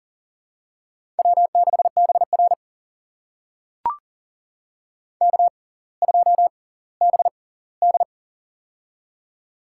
Morse code, heard as a keyed tone, about 40 wpm, spaced as for 20 wpm, 700 Hz. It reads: W6BR K2BD
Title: USA Call Signs - 6x Character Spacing - 40wpm